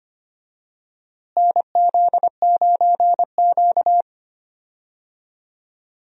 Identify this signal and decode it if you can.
Morse code, heard as a keyed tone, about 25 wpm, 700 Hz. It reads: NZ9Q
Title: USA Call Signs - 4 Characters - Repeated 3x - 25wpm